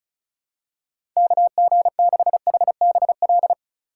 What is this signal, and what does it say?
Morse code, heard as a keyed tone, about 36 wpm, 700 Hz. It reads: KG6HBL
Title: USA Call Signs - 4x Character Spacing - 35wpm